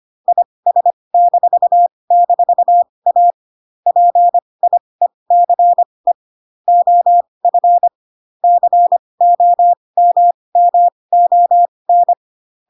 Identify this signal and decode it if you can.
Morse code, heard as a keyed tone, about 25 wpm, spaced as for 20 wpm, 700 Hz. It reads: IS--A PIECE OF COMMON
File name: Pinocchio / Pinocchio ch1 15 25 ik1hge